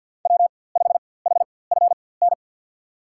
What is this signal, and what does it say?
Morse code, heard as a keyed tone, about 52 wpm, spaced as for 23 wpm, 700 Hz. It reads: W5HFN
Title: USA Call Signs - 5 Characters - 4x Character Spacing - 50wpm